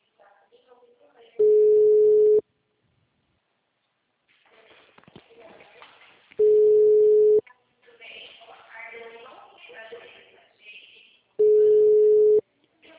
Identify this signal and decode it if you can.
Ringback tone (Europe, 425 Hz), with speech.